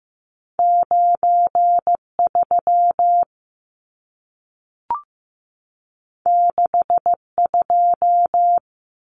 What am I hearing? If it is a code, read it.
Morse code, heard as a keyed tone, about 15 wpm, 700 Hz. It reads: 93 62
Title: Numbers - 2-Digit 15wpm